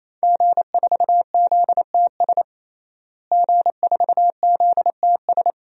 Morse code, heard as a keyed tone, about 27 wpm, 700 Hz. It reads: G4ZTS G4ZTS
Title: DXCC-100 Call Signs - 5 Characters - Repeated 3x - 28wpm